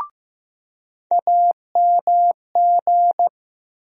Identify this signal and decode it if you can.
Morse code, heard as a keyed tone, about 15 wpm, 700 Hz. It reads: AMG
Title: USA Call Sign Postfixes - 3 Letters 15wpm